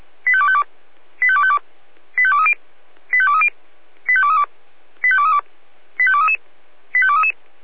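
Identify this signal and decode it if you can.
Five-tone selective call (ZVEI), heard as quick runs of five tones, tones of about 70 ms. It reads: 85252 85252 85230 85230 85232 85232 85233 85233
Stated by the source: ZVEI five-tone selective call received on an SDR.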